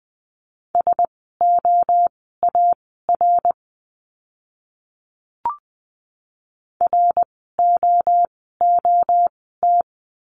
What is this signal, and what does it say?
Morse code, heard as a keyed tone, about 20 wpm, spaced as for 14 wpm, 700 Hz. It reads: SOAR ROOT
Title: 054 Lesson 10f - Letter r - Words 20wpm